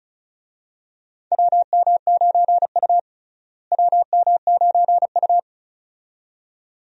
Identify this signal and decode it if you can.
Morse code, heard as a keyed tone, about 36 wpm, 700 Hz. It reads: WM9U WM9U
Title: USA Call Signs - 4 Characters - Repeated 2x - 35wpm